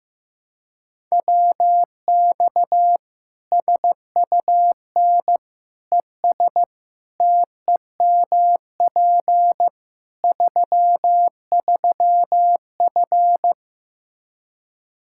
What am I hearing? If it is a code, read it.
Morse code, heard as a keyed tone, about 15 wpm, 700 Hz. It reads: WX SUN ES TEMP 33F
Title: K1USN SST QSO Practice 15wpm